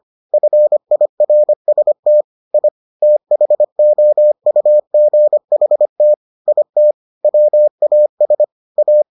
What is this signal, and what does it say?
Morse code, heard as a keyed tone, about 25 wpm, 600 Hz. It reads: FIRST I THOUGHT IT WAS A